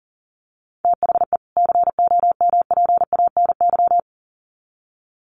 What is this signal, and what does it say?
Morse code, heard as a keyed tone, about 38 wpm, 700 Hz. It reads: THE COMPANY
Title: Sets of 2 Words 40wpm